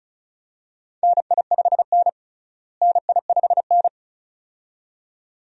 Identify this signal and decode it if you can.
Morse code, heard as a keyed tone, about 36 wpm, 700 Hz. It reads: NI5N NI5N
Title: USA Call Signs - 4 Characters - Repeated 2x - 35wpm